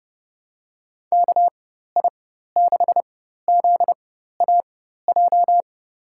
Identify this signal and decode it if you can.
Morse code, heard as a keyed tone, about 30 wpm, spaced as for 14 wpm, 700 Hz. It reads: KI6ZAJ
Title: USA Call Signs - 6 Characters - 4x Character Spacing - 30wpm